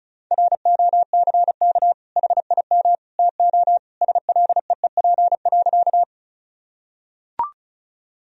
Morse code, heard as a keyed tone, about 34 wpm, 700 Hz. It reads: ROCK HIM TO SLEEP.
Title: Sentences from Top 1000 Words - Review Part 2 - 35wpm